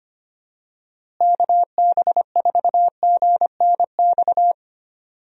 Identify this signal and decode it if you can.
Morse code, heard as a keyed tone, about 25 wpm, 700 Hz. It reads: KB4GNX
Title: USA Call Signs - 6 Characters - Repeated 2x - 25wpm